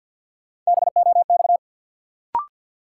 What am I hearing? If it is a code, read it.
Morse code, heard as a keyed tone, about 53 wpm, 700 Hz. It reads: BOX
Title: Sentences from Top 500 Words - New Word Drill - 50wpm